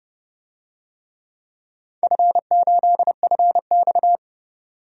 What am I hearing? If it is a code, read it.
Morse code, heard as a keyed tone, about 30 wpm, 700 Hz. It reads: F8FX
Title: DXCC-100 Call Signs - 4 Characters - 30wpm